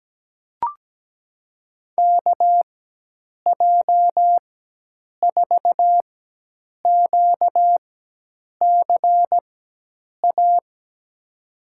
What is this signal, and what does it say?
Morse code, heard as a keyed tone, about 17 wpm, spaced as for 8 wpm, 700 Hz. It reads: KJ4QCA
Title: USA Call Signs - 6 Characters - 4x Character Spacing - 17wpm